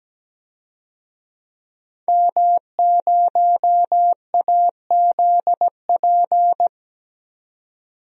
Morse code, heard as a keyed tone, about 17 wpm, 700 Hz. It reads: M0AZP
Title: DXCC-100 Call Signs - 5 Characters - 17wpm